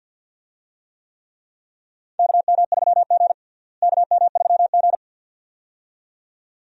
Morse code, heard as a keyed tone, about 47 wpm, 700 Hz. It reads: KM3G KM3G
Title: USA Call Signs - 4 Characters - Repeated 2x - 50wpm